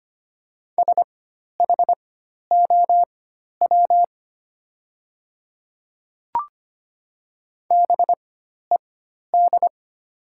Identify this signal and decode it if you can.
Morse code, heard as a keyed tone, about 25 wpm, spaced as for 12 wpm, 700 Hz. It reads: SHOW BED
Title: 4x Character Spacing - Top 500 Words - 25wpm